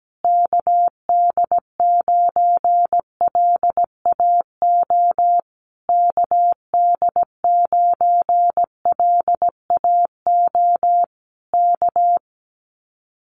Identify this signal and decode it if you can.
Morse code, heard as a keyed tone, about 17 wpm, 700 Hz. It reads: KD9LAO KD9LAO K
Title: Common QSO Phrases Encore 17wpm